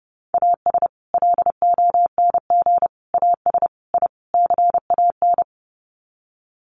Morse code, heard as a keyed tone, about 30 wpm, 700 Hz. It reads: AS LONG AS I CAN
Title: Sets of 5 Words Encore2 30wpm